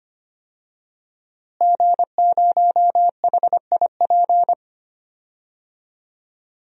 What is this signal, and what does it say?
Morse code, heard as a keyed tone, about 25 wpm, 700 Hz. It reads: G0HIP
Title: DXCC-100 Call Signs - 5 Characters - Repeated 3x - 25wpm